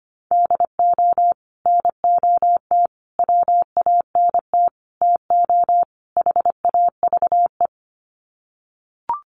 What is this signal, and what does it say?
Morse code, heard as a keyed tone, about 25 wpm, 700 Hz. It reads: DO NOT WANT TO HAVE
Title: Sets of 5 Words 25wpm